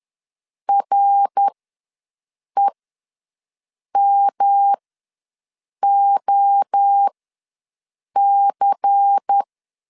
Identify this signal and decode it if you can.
Morse code, heard as a keyed tone, about 10 wpm, spaced as for 6 wpm, 800 Hz. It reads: REMOC
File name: LEZIONI / 7 GF9Z / 7 GF ws 10wpm 3918